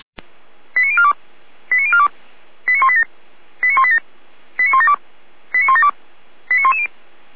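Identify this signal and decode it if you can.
Five-tone selective call (ZVEI), heard as quick runs of five tones, tones of about 70 ms. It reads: 89042 89042 89187 89187 89172 89172 89110
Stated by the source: Five-tone ZVEI selective call received by SDR, from the Feuerwehr Orlamünde (fire brigade) audio server.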